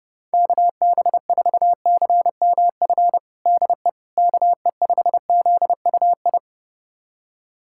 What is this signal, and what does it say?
Morse code, heard as a keyed tone, about 30 wpm, 700 Hz. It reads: KB4CMF DE KE5ZUI
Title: USA Callsign DE Callsign 30wpm